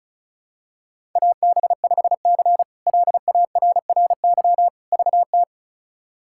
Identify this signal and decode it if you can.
Morse code, heard as a keyed tone, about 34 wpm, 700 Hz. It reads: AB5C LARRY VT
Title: K1USN SST QSO Practice - 35wpm